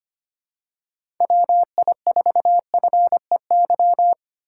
Morse code, heard as a keyed tone, about 25 wpm, 700 Hz. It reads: WI4FEY
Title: USA Call Signs - 6 Characters - 8x Character Spacing - 25wpm